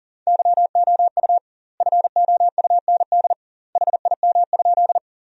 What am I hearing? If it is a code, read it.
Morse code, heard as a keyed tone, about 40 wpm, 700 Hz. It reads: YOU FOUND HIM?
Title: 2x Word-Spacing - Sentences from Top 300 Words - 40wpm